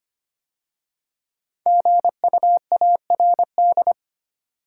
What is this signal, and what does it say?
Morse code, heard as a keyed tone, about 25 wpm, 700 Hz. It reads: GUARD
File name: 5 Letter Words - First 3 Letters Given - 25wpm